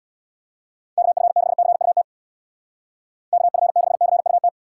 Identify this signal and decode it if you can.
Morse code, heard as a keyed tone, about 62 wpm, 700 Hz. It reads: KF6CUT KF6CUT
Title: Speed Racing - USA 2x3 Call Signs 50wpm